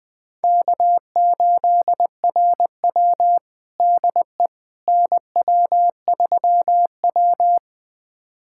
Morse code, heard as a keyed tone, about 20 wpm, 700 Hz. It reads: K8RW DE NW3W
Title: USA Callsign DE Callsign - 4 Characters - 20wpm